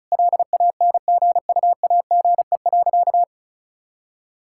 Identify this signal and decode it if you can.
Morse code, heard as a keyed tone, about 36 wpm, 700 Hz. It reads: LANGUAGE.
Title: Sentences from Top 1600 Words 35wpm